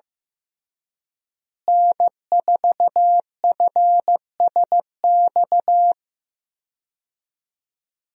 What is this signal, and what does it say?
Morse code, heard as a keyed tone, about 15 wpm, 700 Hz. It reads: N4FSX
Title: USA Call Signs - 5 Characters - Repeated 3x - 15wpm